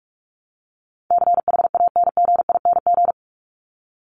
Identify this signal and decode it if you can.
Morse code, heard as a keyed tone, about 45 wpm, 700 Hz. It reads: CHANGING
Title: ING Suffix - Top 100 Words with Base Words 45wpm